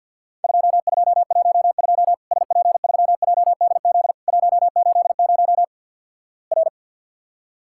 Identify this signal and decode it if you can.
Morse code, heard as a keyed tone, about 50 wpm, 700 Hz. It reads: 2212 SP3JDZ 180 R
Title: Worked All Europe DX Contest - QTC Exchange - 50wpm